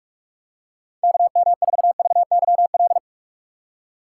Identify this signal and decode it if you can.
Morse code, heard as a keyed tone, about 44 wpm, 700 Hz. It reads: KM4VYL
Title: USA Call Signs - 6 Characters - 2x Character Spacing - 45wpm